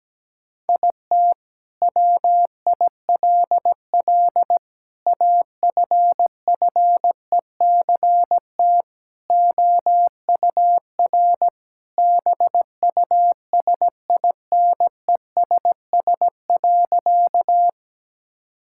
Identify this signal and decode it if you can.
Morse code, heard as a keyed tone, about 17 wpm, 700 Hz. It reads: IT WILL AFFECT OUR BUSINESS.
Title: Sentences from Top 1000 Words - Review Part 6 - 17wpm